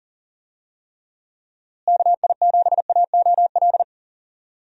Morse code, heard as a keyed tone, about 40 wpm, 700 Hz. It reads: KI7AOL
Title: USA Call Signs - 6 Characters - Repeated 3x - 40wpm